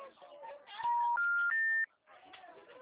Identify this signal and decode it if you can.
Special information tone, with speech.
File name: out- -2017- - - .30414